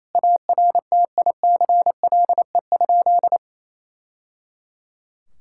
Morse code, heard as a keyed tone, about 28 wpm, 700 Hz. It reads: ARTICLE?
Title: Sentences from Top 1000 Words - Review Part 5 - 28wpm